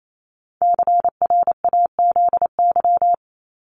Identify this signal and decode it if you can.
Morse code, heard as a keyed tone, about 28 wpm, 700 Hz. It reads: CRAZY
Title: Top 1501-1750 Words 28wpm